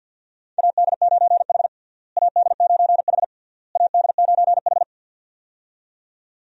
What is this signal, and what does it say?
Morse code, heard as a keyed tone, about 49 wpm, 700 Hz. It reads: AD9H AD9H AD9H
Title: USA Call Signs - 4 Characters - Repeated 3x - 50wpm